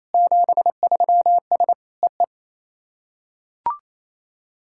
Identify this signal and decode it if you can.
Morse code, heard as a keyed tone, about 28 wpm, 690 Hz. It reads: 73S EE
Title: SOTA QSO Practice 28wpm